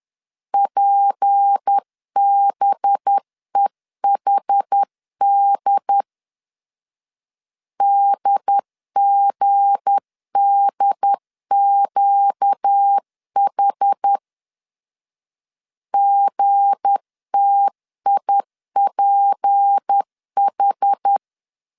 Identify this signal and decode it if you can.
Morse code, heard as a keyed tone, about 10 wpm, 800 Hz. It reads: PBEHD DGDQH GTIPH
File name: LEZIONI / 7 GF9Z / 7 G 10wpm 2501